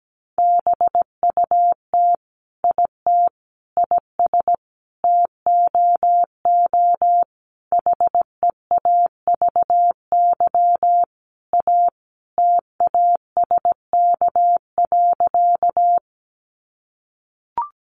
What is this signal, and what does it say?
Morse code, heard as a keyed tone, about 17 wpm, 700 Hz. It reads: BUT IT IS TOO HEAVY A TASK.
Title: Sentences from Top 1100 Words 17wpm